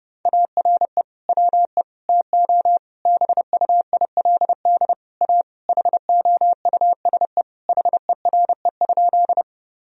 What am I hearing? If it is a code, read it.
Morse code, heard as a keyed tone, about 32 wpm, 700 Hz. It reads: ARE WE TO BUILD A HOUSE HERE?
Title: Sentences from Top 1000 Words - Review Part 1 - 30wpm